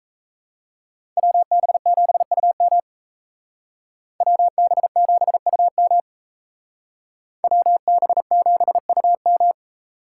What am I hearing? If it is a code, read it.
Morse code, heard as a keyed tone, about 32 wpm, 700 Hz. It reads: WB7UM WB7UM WB7UM
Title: Speed Racing - USA 2x2 Call Signs 28wpm